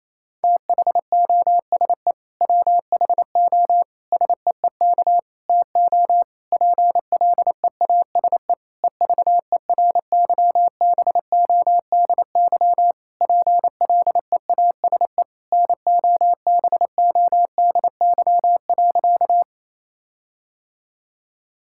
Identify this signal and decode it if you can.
Morse code, heard as a keyed tone, about 28 wpm, 700 Hz. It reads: THOSE WHO SEEK TO PLEASE EVERYBODY PLEASE NOBODY.
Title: Sentences from Top 1600 Words 28wpm